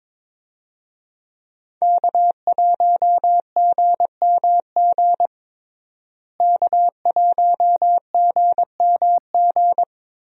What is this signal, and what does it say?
Morse code, heard as a keyed tone, about 22 wpm, 700 Hz. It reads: K1GMG K1GMG
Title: USA Call Signs - 5 Characters - Repeated 2x - 22wpm